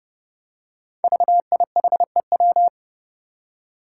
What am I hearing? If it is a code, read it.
Morse code, heard as a keyed tone, about 30 wpm, 700 Hz. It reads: VIHEW
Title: Code Groups - 5 Letter:Numbers - 30wpm